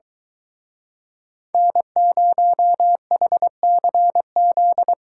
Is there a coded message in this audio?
Yes: Morse code from 1.5 to 4.9 s, heard as a keyed tone, about 23 wpm, 700 Hz. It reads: N0HCZ